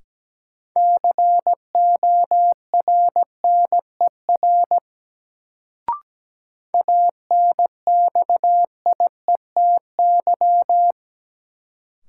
Morse code, heard as a keyed tone, about 17 wpm, 700 Hz. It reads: CORNER ANXIETY
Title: Sentences from Top 1600 Words - New Word Drill Alt - 17wpm